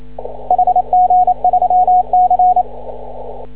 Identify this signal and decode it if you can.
Morse code, heard as a keyed tone, about 28 wpm, 730 Hz. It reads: HG3C
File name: CW / ha6oi 2